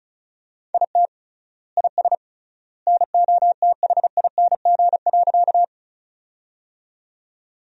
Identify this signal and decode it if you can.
Morse code, heard as a keyed tone, about 36 wpm, 700 Hz. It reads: IT IS NOTHING.
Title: 3x Word-Spacing - Sentences from Top 500 Words - 35wpm